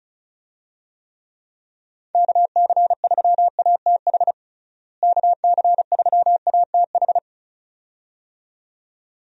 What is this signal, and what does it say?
Morse code, heard as a keyed tone, about 36 wpm, 700 Hz. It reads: KC3ATH KC3ATH
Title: USA Call Signs - 6 Characters - Repeated 2x - 35wpm